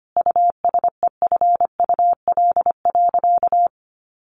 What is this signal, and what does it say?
Morse code, heard as a keyed tone, about 25 wpm, 700 Hz. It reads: USEFUL.
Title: Sentences from Top 1400 Words 25wpm